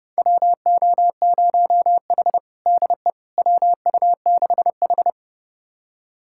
Morse code, heard as a keyed tone, about 32 wpm, 700 Hz. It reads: WO0H DE WU6H
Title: USA Callsign DE Callsign - 4 Characters - 30wpm